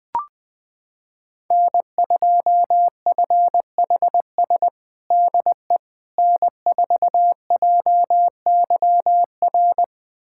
Morse code, heard as a keyed tone, about 21 wpm, 700 Hz. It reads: N2FHS DE N4JYR
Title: USA Callsign DE Callsign - 5 Characters - 20wpm